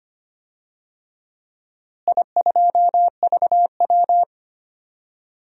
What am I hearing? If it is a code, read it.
Morse code, heard as a keyed tone, about 25 wpm, 700 Hz. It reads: I2VW
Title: DXCC-100 Call Signs - 4 Characters - Repeated 3x - 25wpm